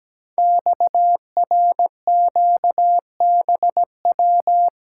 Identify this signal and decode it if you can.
Morse code, heard as a keyed tone, about 17 wpm, 700 Hz. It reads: XRQBW
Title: Code Groups - 5 Letters - 17wpm